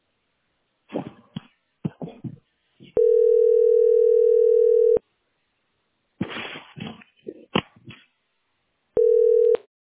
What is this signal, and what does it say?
Ringback tone (North America), 440 + 480 Hz.